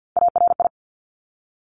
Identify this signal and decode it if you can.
Morse code, heard as a keyed tone, about 50 wpm, 700 Hz. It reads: ARI
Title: Top Three Letter Combinations in Words 50wpm